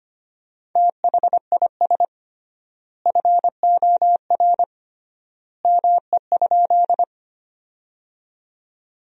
Morse code, heard as a keyed tone, about 25 wpm, 700 Hz. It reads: THIS FOR ME?
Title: 3x Word-Spacing - Sentences from Top 200 Words - 25wpm